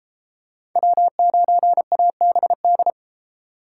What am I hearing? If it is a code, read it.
Morse code, heard as a keyed tone, about 32 wpm, 700 Hz. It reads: W9ABD